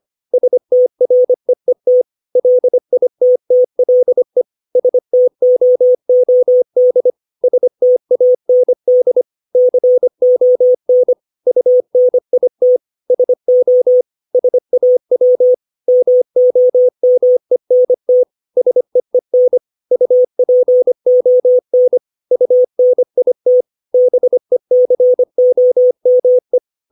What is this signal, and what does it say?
Morse code, heard as a keyed tone, about 25 wpm, 500 Hz. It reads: STREET LITTLE STOOD STAND CON UNIT SO SAW MOMENT SEEN UPON UNIT BECOME
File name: 632 words / 632 Words 25WPM